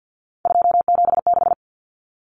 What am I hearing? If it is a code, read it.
Morse code, heard as a keyed tone, about 50 wpm, 700 Hz. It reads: 276
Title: Common Cut Numbers - 3-Digit 50wpm